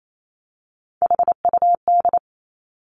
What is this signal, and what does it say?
Morse code, heard as a keyed tone, about 28 wpm, 700 Hz. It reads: HUD
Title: Three Challenging Letter-Number Combinations 28wpm